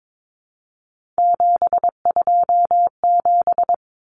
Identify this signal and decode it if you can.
Morse code, heard as a keyed tone, about 22 wpm, 700 Hz. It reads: 727
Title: Numbers - 3-Digit 22wpm